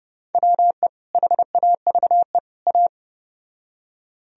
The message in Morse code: WE HAVE A

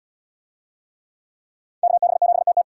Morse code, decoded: KB7TT

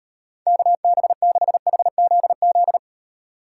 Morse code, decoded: KB6HZZ

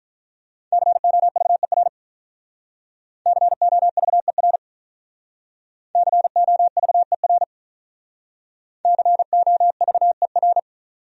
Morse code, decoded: COVER COVER COVER COVER

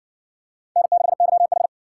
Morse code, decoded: N6YH

